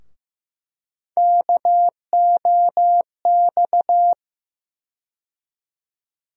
Morse code, heard as a keyed tone, about 15 wpm, 700 Hz. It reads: KOX
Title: USA Call Sign Postfixes - 3 Letters 15wpm